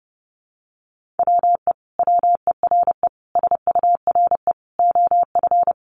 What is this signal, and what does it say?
Morse code, heard as a keyed tone, about 30 wpm, 700 Hz. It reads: WE WERE SURE OF
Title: Sentences from Top 1100 Words 30wpm